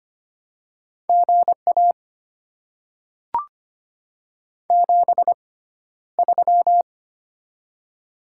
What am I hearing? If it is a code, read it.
Morse code, heard as a keyed tone, about 25 wpm, spaced as for 13 wpm, 700 Hz. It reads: GA 73
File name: Top 20 QSO Elements - 6x Character Spacing - 25wpm